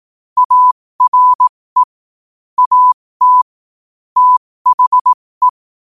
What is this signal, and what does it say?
Morse code, heard as a keyed tone, about 18 wpm, spaced as for 15 wpm, 1000 Hz. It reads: ARE AT THE
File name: qotdfiles / 15wpm